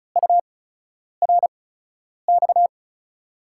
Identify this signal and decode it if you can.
Morse code, heard as a keyed tone, about 36 wpm, spaced as for 10 wpm, 700 Hz. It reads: URX
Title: USA Call Signs - 5 Characters - 8x Character Spacing - 35wpm